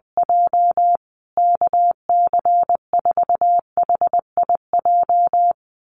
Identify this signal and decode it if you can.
Morse code, heard as a keyed tone, about 20 wpm, 700 Hz. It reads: J KC4HIJ
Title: USA Call Signs Repeated 2x 20wpm